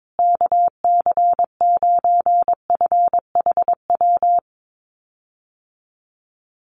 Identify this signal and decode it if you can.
Morse code, heard as a keyed tone, about 22 wpm, 700 Hz. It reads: KC9FHW
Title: USA Call Signs Repeated 3x 22wpm